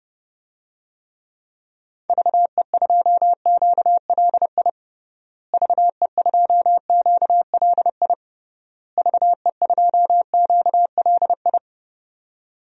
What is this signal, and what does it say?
Morse code, heard as a keyed tone, about 32 wpm, 700 Hz. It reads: VE2QLI VE2QLI VE2QLI
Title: DXCC-100 Call Signs - 6 Characters - Repeated 3x - 30wpm